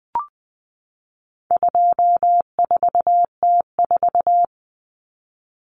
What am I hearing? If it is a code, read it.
Morse code, heard as a keyed tone, about 20 wpm, 700 Hz. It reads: 24T4
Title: Common Cut Numbers - 4-Digit 20wpm